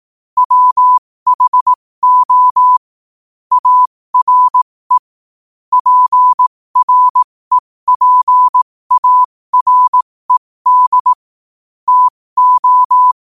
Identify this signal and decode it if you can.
Morse code, heard as a keyed tone, about 18 wpm, spaced as for 15 wpm, 1000 Hz. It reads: WHO ARE PREPARED TO
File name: qotdfiles / 15wpm